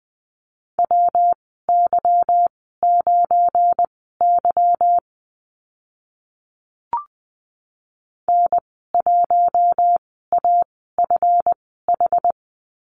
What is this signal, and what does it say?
Morse code, heard as a keyed tone, about 20 wpm, spaced as for 14 wpm, 700 Hz. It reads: WY9Y N1AFH
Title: 213 Lesson 34g - Number 8 - Call Signs 20wpm